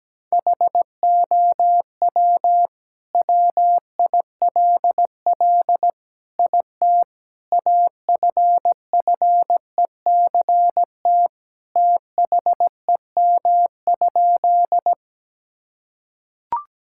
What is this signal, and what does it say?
Morse code, heard as a keyed tone, about 17 wpm, 700 Hz. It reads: HOW WILL IT AFFECT THEM?
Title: Sentences from Top 1000 Words - Review Part 6 - 17wpm